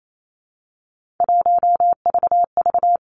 Morse code, heard as a keyed tone, about 28 wpm, 700 Hz. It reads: 1VV
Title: Three Challenging Letter-Number Combinations 28wpm